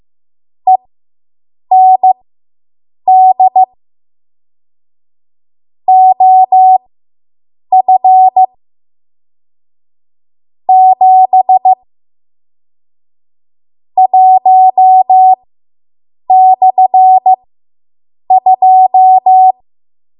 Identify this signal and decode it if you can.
Morse code, heard as a keyed tone, about 15 wpm, spaced as for 7 wpm, 750 Hz. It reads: END OF 7 1/2